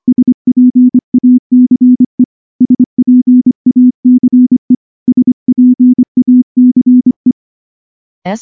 Morse code, heard as a keyed tone, about 24 wpm, 250 Hz. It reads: SPACE SPACE SPACE
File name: morse-training-25WPM